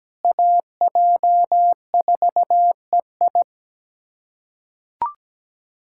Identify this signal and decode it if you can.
Morse code, heard as a keyed tone, about 17 wpm, 700 Hz. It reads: AJ4EI
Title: USA Callsign DE Callsign - 5 Characters - 17wpm